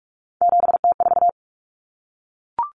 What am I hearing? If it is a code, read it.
Morse code, heard as a keyed tone, about 44 wpm, 700 Hz. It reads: 7T4
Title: Common Cut Numbers - 3-Digit 45wpm